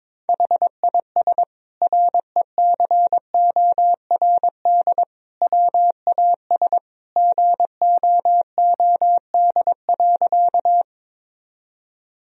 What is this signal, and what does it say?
Morse code, heard as a keyed tone, about 22 wpm, 700 Hz. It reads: HIS RECORD WAS GOOD.
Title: Sentences from Top 1000 Words - Review Part 3 - 22wpm